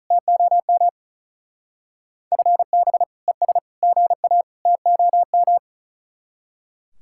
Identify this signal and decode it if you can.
Morse code, heard as a keyed tone, about 42 wpm, 700 Hz. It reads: TOM FB ES GA TOM
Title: Speed-Racing Common QSO Phrases - 35wpm